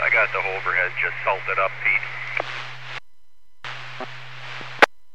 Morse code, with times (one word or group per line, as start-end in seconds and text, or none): none